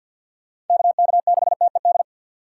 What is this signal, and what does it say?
Morse code, heard as a keyed tone, about 48 wpm, 700 Hz. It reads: KK6TED
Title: USA Call Signs - 6 Characters - 4x Character Spacing - 50wpm